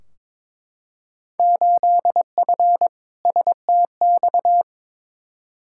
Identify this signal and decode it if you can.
Morse code, heard as a keyed tone, about 22 wpm, 700 Hz. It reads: 8F STX
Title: Field Day Exchanges Encore - 22wpm